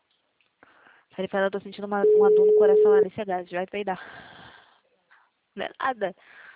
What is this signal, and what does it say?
Ringback tone (Europe, 425 Hz), with speech.